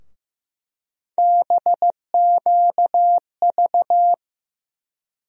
Morse code, heard as a keyed tone, about 15 wpm, 700 Hz. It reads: BQV
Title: USA Call Sign Postfixes - 3 Letters 15wpm